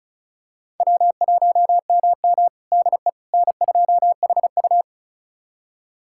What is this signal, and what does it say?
Morse code, heard as a keyed tone, about 36 wpm, 700 Hz. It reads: W1MM DE N2HU
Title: USA Callsign DE Callsign - 4 Characters - 35wpm